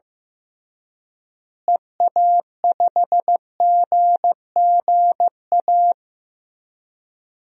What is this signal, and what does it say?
Morse code, heard as a keyed tone, about 15 wpm, 700 Hz. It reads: EA5GGA
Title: DXCC-100 Call Signs - 6 Characters - Repeated 3x - 15wpm